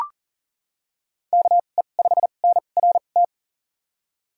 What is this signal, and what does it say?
Morse code, heard as a keyed tone, about 38 wpm, spaced as for 28 wpm, 700 Hz. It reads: KE5NRT
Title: USA Call Signs - 6 Characters - 2x Character Spacing - 40wpm